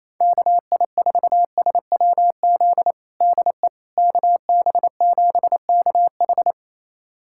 Morse code, heard as a keyed tone, about 28 wpm, 700 Hz. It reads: KI4SWZ DE KB7KH
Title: USA Callsign DE Callsign 28wpm